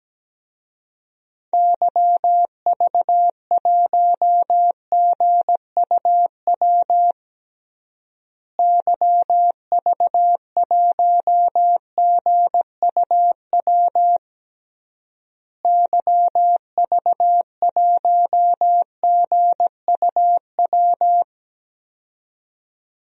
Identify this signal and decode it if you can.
Morse code, heard as a keyed tone, about 17 wpm, 700 Hz. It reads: YV1GUW YV1GUW YV1GUW
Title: DXCC-100 Call Signs - 6 Characters - Repeated 3x - 17wpm